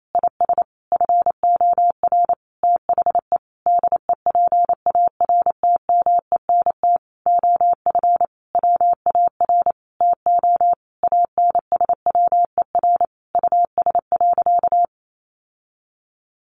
Morse code, heard as a keyed tone, about 28 wpm, 700 Hz. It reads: IS FOR THE DEPARTMENT OF WAR TO ANSWER US.
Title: Sentences from Top 1500 Words 28wpm